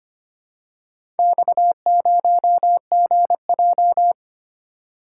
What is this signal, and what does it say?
Morse code, heard as a keyed tone, about 26 wpm, 700 Hz. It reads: X0GJ
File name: practice / 25wpm / Four Letter-Number Combinations - 25wpm